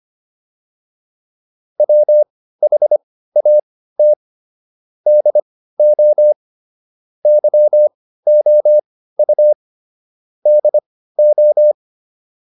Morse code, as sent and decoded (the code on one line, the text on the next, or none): .-- .... .- - / -.. --- / -.-- --- ..- / -.. ---
WHAT DO YOU DO